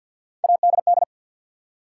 Morse code, heard as a keyed tone, about 52 wpm, 700 Hz. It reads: ADD
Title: Three Letter Words - 50wpm